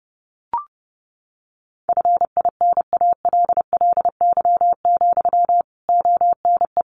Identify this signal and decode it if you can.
Morse code, heard as a keyed tone, about 30 wpm, 700 Hz. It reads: FINALLY, ONE
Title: Sentences from Top 1300 Words 30wpm